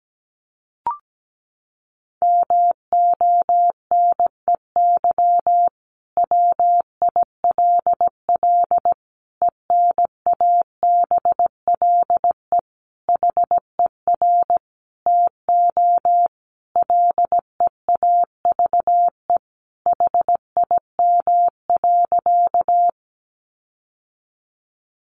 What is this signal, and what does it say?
Morse code, heard as a keyed tone, about 17 wpm, 700 Hz. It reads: MONEY WILL ENABLE HER TO LEAVE HIM.
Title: Sentences from Top 1500 Words 17wpm